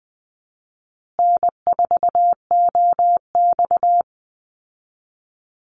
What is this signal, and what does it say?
Morse code, heard as a keyed tone, about 20 wpm, 700 Hz. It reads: N4OX
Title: Easy Call Signs Repeated 3x - 20wpm